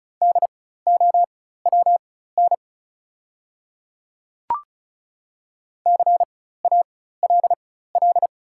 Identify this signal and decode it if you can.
Morse code, heard as a keyed tone, about 36 wpm, spaced as for 16 wpm, 690 Hz. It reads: DOWN CALL
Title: 4x Character Spacing - Top 100 Words - 35wpm